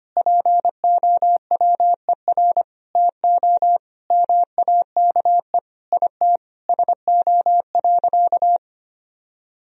Morse code, heard as a keyed tone, about 25 wpm, 700 Hz. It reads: POWER TO MAKE IT SO.
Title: Sentences from Top 1000 Words - Review Part 3 - 25wpm